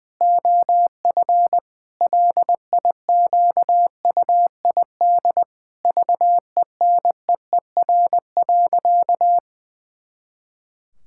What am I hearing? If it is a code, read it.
Morse code, heard as a keyed tone, about 20 wpm, 700 Hz. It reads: OF LIQUID VENEER.
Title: Pangram Challenge v2 - 20wpm